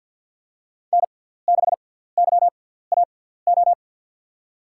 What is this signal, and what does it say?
Morse code, heard as a keyed tone, about 52 wpm, spaced as for 17 wpm, 700 Hz. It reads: N6YAO
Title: USA Call Signs - 5 Characters - 6x Character Spacing - 50wpm